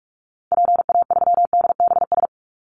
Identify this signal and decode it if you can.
Morse code, heard as a keyed tone, about 46 wpm, 700 Hz. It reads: PA3DBS
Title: Call Signs - 45wpm